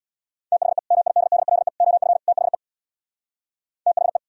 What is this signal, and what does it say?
Morse code, heard as a keyed tone, about 77 wpm, 700 Hz. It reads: THE MIDDLE OF THE THE